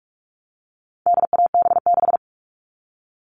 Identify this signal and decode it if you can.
Morse code, heard as a keyed tone, about 44 wpm, 700 Hz. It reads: DAB6
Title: All Cut Numbers - 4-Digit 45wpm